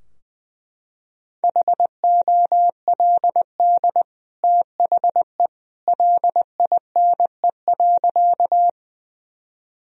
Morse code, heard as a keyed tone, about 21 wpm, 700 Hz. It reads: HOLD THE LINE.
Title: Sentences from Top 1000 Words - Review Part 2 - 20wpm